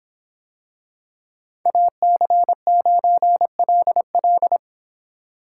Morse code, heard as a keyed tone, about 26 wpm, 700 Hz. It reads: AC9LL